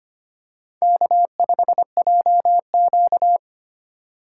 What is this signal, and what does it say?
Morse code, heard as a keyed tone, about 25 wpm, 700 Hz. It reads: K5JQ
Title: USA Call Signs - 4 Characters - 3x Character Spacing - 25wpm